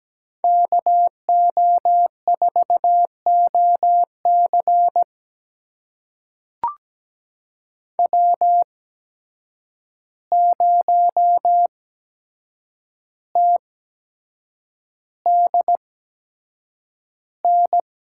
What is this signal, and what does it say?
Morse code, heard as a keyed tone, about 17 wpm, spaced as for 7 wpm, 700 Hz. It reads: KO4OC W0TDN
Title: USA Call Signs - 5 Characters - 8x Character Spacing - 17wpm